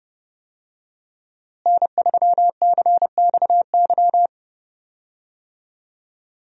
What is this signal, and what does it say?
Morse code, heard as a keyed tone, about 32 wpm, 700 Hz. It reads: N3CXY
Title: USA Call Signs - 5 Characters - Repeated 3x - 30wpm